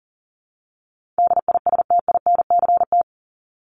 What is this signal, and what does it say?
Morse code, heard as a keyed tone, about 40 wpm, 700 Hz. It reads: DISTINCT
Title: Top 1501-1750 Words 40wpm